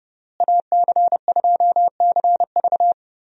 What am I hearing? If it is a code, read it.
Morse code, heard as a keyed tone, about 32 wpm, 700 Hz. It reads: AC2CV
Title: USA Call Signs - 5 Characters - Repeated 3x - 30wpm